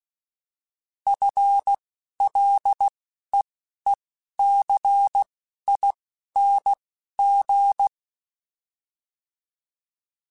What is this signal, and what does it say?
Morse code, heard as a keyed tone, about 16 wpm, spaced as for 11 wpm, 800 Hz. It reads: FLEECING
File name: FL / FL mp3s / FL Prefixes 3x2 sp @10